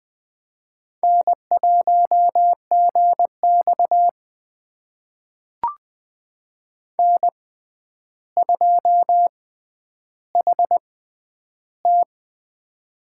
Morse code, heard as a keyed tone, about 20 wpm, spaced as for 10 wpm, 700 Hz. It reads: N1GX N2HT
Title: USA Call Signs - 4 Characters - 6x Character Spacing - 20wpm